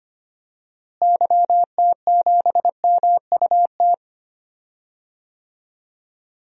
Morse code, heard as a keyed tone, about 25 wpm, 700 Hz. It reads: YT7MUT
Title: DXCC-100 Call Signs - 6 Characters - Repeated 3x - 25wpm